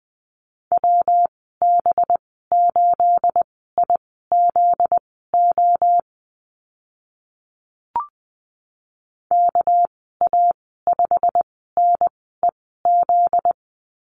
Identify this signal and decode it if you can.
Morse code, heard as a keyed tone, about 20 wpm, spaced as for 15 wpm, 700 Hz. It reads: WB8IZO KA5NEZ
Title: 245 Lesson 39g - Letter z - Call Signs 20wpm